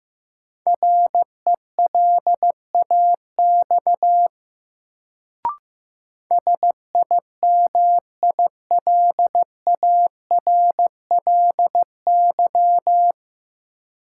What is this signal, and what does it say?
Morse code, heard as a keyed tone, about 15 wpm, 700 Hz. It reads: RELAX SIMILARLY
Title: Sentences from Top 1400 Words - New Word Drill - 15wpm